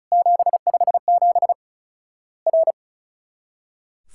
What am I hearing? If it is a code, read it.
Morse code, heard as a keyed tone, about 34 wpm, 700 Hz. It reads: 757 R
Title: Worked All Europe DX Contest - QTC Exchange - 35wpm